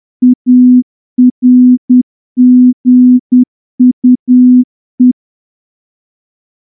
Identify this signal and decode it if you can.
Morse code, heard as a keyed tone, about 10 wpm, 250 Hz. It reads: ARGUE
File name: morse-training-10WPM